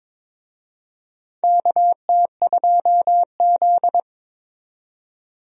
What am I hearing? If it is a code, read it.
Morse code, heard as a keyed tone, about 22 wpm, 700 Hz. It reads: KT2Z
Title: USA Call Signs - 4 Characters - 22wpm